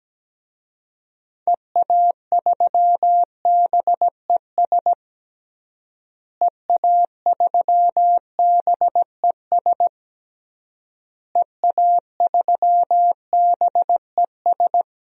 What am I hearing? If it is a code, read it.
Morse code, heard as a keyed tone, about 17 wpm, 700 Hz. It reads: EA3BES EA3BES EA3BES
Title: DXCC-100 Call Signs - 6 Characters - Repeated 3x - 17wpm